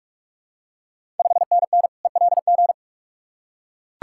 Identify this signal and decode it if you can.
Morse code, heard as a keyed tone, about 41 wpm, 700 Hz. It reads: 5NN ELG
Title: Ontario QSO Party Exchange 45wpm